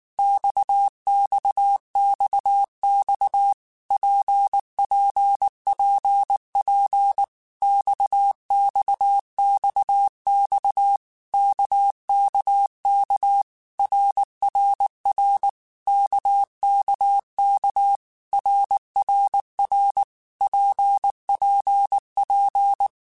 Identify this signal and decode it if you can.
Morse code, heard as a keyed tone, about 19 wpm, 800 Hz. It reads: XXXX PPPP XXXX KKK RRR KKK RRR PPP
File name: Basic Letter Groups By7Groups / RKPX / RKPX@18